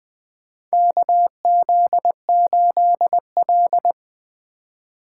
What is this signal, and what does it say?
Morse code, heard as a keyed tone, about 20 wpm, 700 Hz. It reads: KZ8L
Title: USA Call Signs - 2x Character Spacing - 20wpm